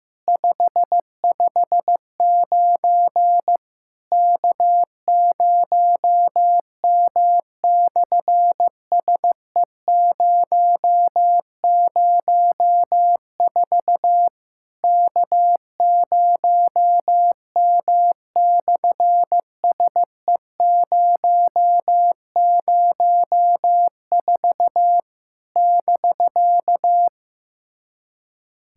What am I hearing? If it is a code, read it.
Morse code, heard as a keyed tone, about 15 wpm, 700 Hz. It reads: 559 K0M/SE004 K0M/SE004 <BK>
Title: SOTA QSO Practice 15wpm